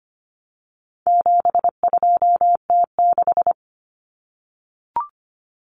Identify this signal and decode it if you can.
Morse code, heard as a keyed tone, about 25 wpm, 700 Hz. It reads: 72T6
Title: Common Cut Numbers - 4-Digit 25wpm